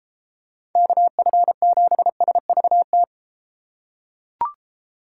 Morse code, heard as a keyed tone, about 32 wpm, 700 Hz. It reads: KF7SVT